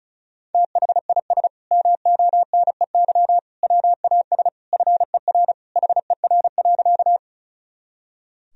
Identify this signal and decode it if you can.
Morse code, heard as a keyed tone, about 34 wpm, 700 Hz. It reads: THIS MONEY WAS FER HER.
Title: Sentences from Top 400 Words and Abbreviations 35wpm